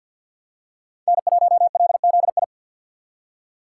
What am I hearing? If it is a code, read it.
Morse code, heard as a keyed tone, about 48 wpm, 700 Hz. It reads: N1LZI
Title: USA Call Signs - 5 Characters - 3x Character Spacing - 50wpm